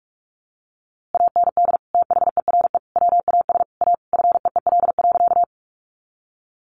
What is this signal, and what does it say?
Morse code, heard as a keyed tone, about 44 wpm, 700 Hz. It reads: AND THERE WAS A FEEL.
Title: A Sound Of Thunder 45wpm